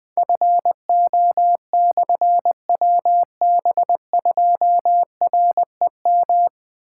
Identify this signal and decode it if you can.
Morse code, heard as a keyed tone, about 20 wpm, 700 Hz. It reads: FO/WB2REM
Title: Difficult Call Signs 20wpm